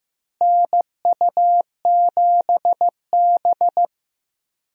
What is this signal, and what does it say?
Morse code, heard as a keyed tone, about 16 wpm, 700 Hz. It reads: NU7B
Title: USA Call Signs - 4 Characters - Repeated 3x - 15wpm